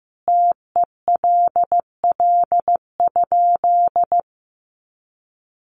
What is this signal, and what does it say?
Morse code, heard as a keyed tone, about 15 wpm, 700 Hz. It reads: TELL?
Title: A Sound Of Thunder 15wpm